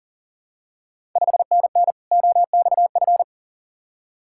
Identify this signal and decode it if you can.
Morse code, heard as a keyed tone, about 37 wpm, 690 Hz. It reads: 5NN OXF
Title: Ontario QSO Party Exchange 40wpm